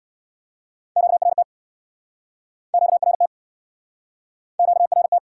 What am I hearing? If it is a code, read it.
Morse code, heard as a keyed tone, about 69 wpm, 700 Hz. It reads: QRT QRT QRT